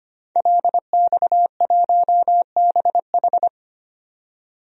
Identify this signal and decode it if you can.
Morse code, heard as a keyed tone, about 25 wpm, 700 Hz. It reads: LX1BH
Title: DXCC-100 Call Signs - 5 Characters - 25wpm